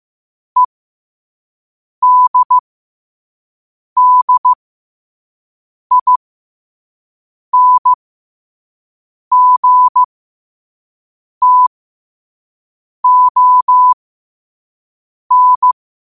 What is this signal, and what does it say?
Morse code, heard as a keyed tone, about 15 wpm, spaced as for 5 wpm, 1000 Hz. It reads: EDDINGTON
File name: qotdfiles / 5wpm